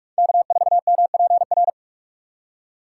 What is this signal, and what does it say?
Morse code, heard as a keyed tone, about 46 wpm, 700 Hz. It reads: K4MPR
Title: USA Call Signs - 5 Characters - 3x Character Spacing - 45wpm